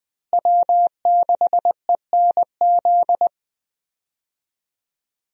Morse code, heard as a keyed tone, about 20 wpm, 700 Hz. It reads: W6ENZ
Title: USA Call Signs - 5 Characters - Repeated 3x - 20wpm